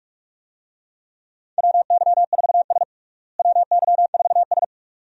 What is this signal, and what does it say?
Morse code, heard as a keyed tone, about 46 wpm, 700 Hz. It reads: WY4S WY4S
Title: USA Call Signs - 4 Characters - Repeated 2x - 45wpm